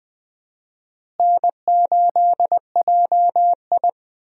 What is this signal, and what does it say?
Morse code, heard as a keyed tone, about 20 wpm, 700 Hz. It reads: N8JI
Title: USA Call Signs - 4 Characters - Repeated 3x - 20wpm